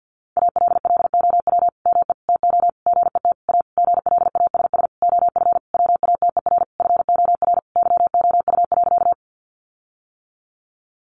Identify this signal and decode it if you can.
Morse code, heard as a keyed tone, about 50 wpm, 700 Hz. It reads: ALLOW ME TO GET A GLASS OF WATER FOR YOU.
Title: Sentences from Top 1300 Words 50wpm